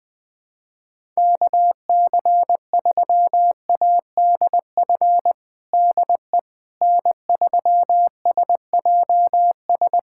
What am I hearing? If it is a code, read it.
Morse code, heard as a keyed tone, about 20 wpm, 700 Hz. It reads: KC3ADF DE N3SJS
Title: USA Callsign DE Callsign 20wpm